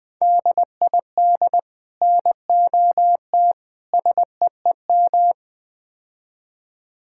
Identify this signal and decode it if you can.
Morse code, heard as a keyed tone, about 20 wpm, 700 Hz. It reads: DID NOT SEEM
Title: Build A Phrase - 4-Word Phrases - 20wpm